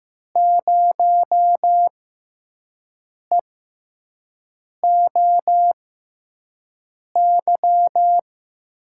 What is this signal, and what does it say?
Morse code, heard as a keyed tone, about 15 wpm, spaced as for 5 wpm, 700 Hz. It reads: 0EOY
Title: USA Call Signs - 5 Characters - 6x Character Spacing - 15wpm